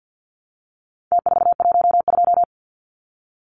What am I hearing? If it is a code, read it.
Morse code, heard as a keyed tone, about 48 wpm, 700 Hz. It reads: T412
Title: Numbers - Contest Style - 4-Digit 50wpm